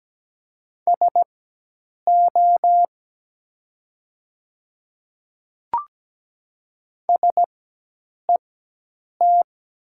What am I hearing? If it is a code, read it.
Morse code, heard as a keyed tone, about 17 wpm, spaced as for 8 wpm, 700 Hz. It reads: SO SET
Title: 4x Character Spacing - Top 200 Words - 17wpm